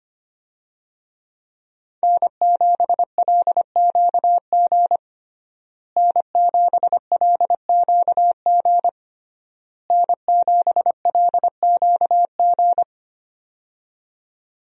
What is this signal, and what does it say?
Morse code, heard as a keyed tone, about 25 wpm, 700 Hz. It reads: N7LQG N7LQG N7LQG
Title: USA Call Signs - 5 Characters - Repeated 3x - 25wpm